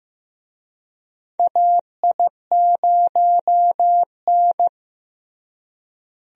Morse code, heard as a keyed tone, about 15 wpm, 700 Hz. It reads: AI0N
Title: USA Call Signs - 4 Characters - Repeated 2x - 15wpm